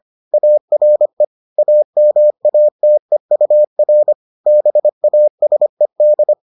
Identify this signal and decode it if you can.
Morse code, heard as a keyed tone, about 25 wpm, 600 Hz. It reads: ARE AMATEUR BASED